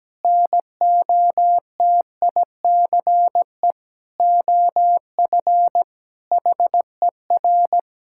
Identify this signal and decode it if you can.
Morse code, heard as a keyed tone, about 17 wpm, 700 Hz. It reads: NOTICE OF HER
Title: Sentences from Top 1000 Words - Review Part 6 - 17wpm